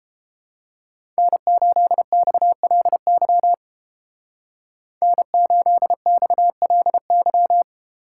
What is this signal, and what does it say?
Morse code, heard as a keyed tone, about 30 wpm, 700 Hz. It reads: N8XLY N8XLY